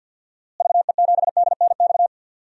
Morse code, heard as a keyed tone, about 52 wpm, 700 Hz. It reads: VE7DNX
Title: DXCC-100 Call Signs - 6 Characters - Repeated 2x - 50wpm